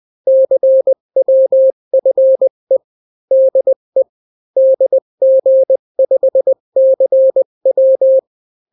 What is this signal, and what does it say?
Morse code, heard as a keyed tone, about 20 wpm, 530 Hz. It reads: CWFE DE DG5CW
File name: daily / 18wpm